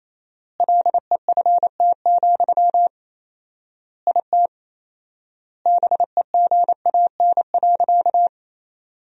Morse code, heard as a keyed tone, about 27 wpm, 700 Hz. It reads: LEFT, IT BEGAN.
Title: 4x Word-Spacing - Sentences from Top 400 Words - 28wpm